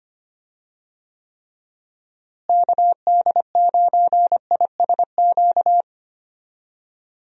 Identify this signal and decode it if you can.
Morse code, heard as a keyed tone, about 25 wpm, 700 Hz. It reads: KD9ISQ
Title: USA Call Signs - 6 Characters - Repeated 3x - 25wpm